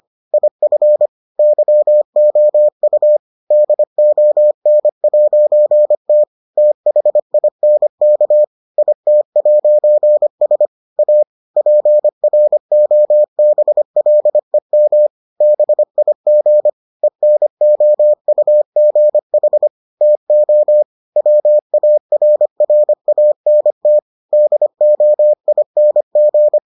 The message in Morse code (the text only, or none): IF YOU DON'T THINK IT'S A PROBLEM BIG ENOUGH TO WARRANT DOING